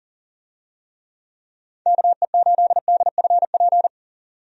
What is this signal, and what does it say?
Morse code, heard as a keyed tone, about 40 wpm, 700 Hz. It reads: KE8DFP
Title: USA Call Signs - 6 Characters - 40wpm